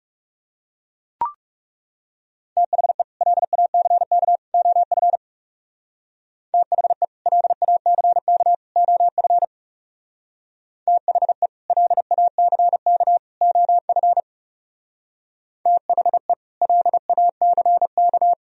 Morse code, heard as a keyed tone, about 36 wpm, 700 Hz. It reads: THE LACK OF THE LACK OF THE LACK OF THE LACK
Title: Speed-Racing - Sets of 3 Words 30wpm